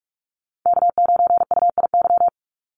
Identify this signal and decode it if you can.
Morse code, heard as a keyed tone, about 44 wpm, 700 Hz. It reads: K9UIY
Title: Call Signs - 45wpm